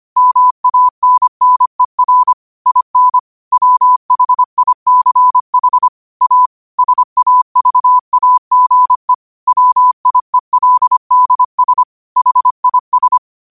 Morse code, heard as a keyed tone, about 25 wpm, 1000 Hz. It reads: MANNER IN WHICH A SAVAGE WIELDS HIS